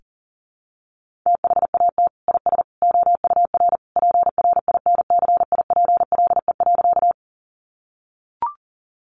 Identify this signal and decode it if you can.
Morse code, heard as a keyed tone, about 40 wpm, 700 Hz. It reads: THAT IS OUR PRINCIPLE.
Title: Sentences from Top 1400 Words 40wpm